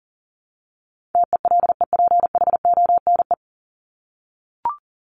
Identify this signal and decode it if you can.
Morse code, heard as a keyed tone, about 40 wpm, 700 Hz. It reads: TELEPHONE
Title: Top 1501-1750 Words 40wpm